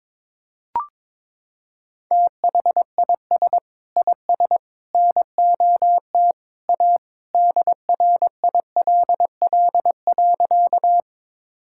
Morse code, heard as a keyed tone, about 22 wpm, 700 Hz. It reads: THIS IS NOT A DRILL.
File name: practice / 22wpm / Sentences with 3-Word Phrases - 22wpm